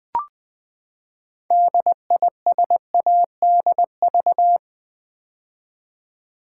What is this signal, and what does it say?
Morse code, heard as a keyed tone, about 20 wpm, 700 Hz. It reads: DISADV
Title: Build A Word - 6 Letters At A Time - 12-Letter Words - 20wpm